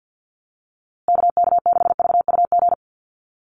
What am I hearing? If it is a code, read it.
Morse code, heard as a keyed tone, about 48 wpm, 700 Hz. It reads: KK6VUG
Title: USA Call Signs Repeated 3x 50wpm